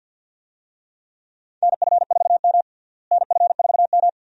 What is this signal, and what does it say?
Morse code, heard as a keyed tone, about 48 wpm, 700 Hz. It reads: NF4M NF4M
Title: USA Call Signs - 4 Characters - Repeated 2x - 50wpm